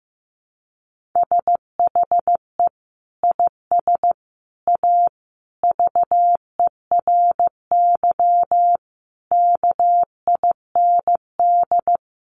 Morse code, heard as a keyed tone, about 15 wpm, 700 Hz. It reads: SHE IS A VERY KIND
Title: Sentences from Top 1100 Words 15wpm